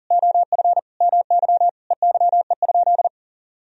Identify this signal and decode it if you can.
Morse code, heard as a keyed tone, about 37 wpm, 700 Hz. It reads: OF MY EYE?
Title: Sentences from Top 500 Words and Abbreviations 40wpm